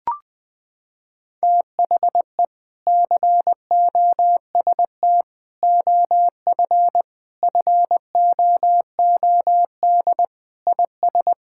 Morse code, heard as a keyed tone, about 20 wpm, 700 Hz. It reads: THE COST OF FOOD IS
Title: Sentences with 3-Word Phrases - 20wpm